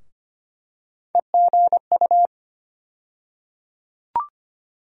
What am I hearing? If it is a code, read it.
Morse code, heard as a keyed tone, about 25 wpm, 700 Hz. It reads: EGU
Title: USA Call Sign Postfixes - 3 Letters 25wpm